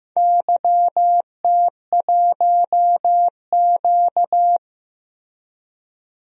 Morse code, heard as a keyed tone, about 15 wpm, 700 Hz. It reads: YT1Q
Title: DXCC-100 Call Signs - 4 Characters - Repeated 3x - 15wpm